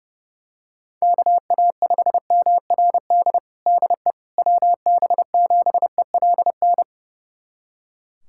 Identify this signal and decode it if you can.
Morse code, heard as a keyed tone, about 30 wpm, 700 Hz. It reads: KA5MRD DE WB7ELN
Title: USA Callsign DE Callsign 30wpm